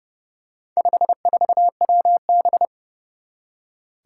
Morse code, heard as a keyed tone, about 28 wpm, 700 Hz. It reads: 54WB
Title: Four Challenging Letter-Number Combinations 30wpm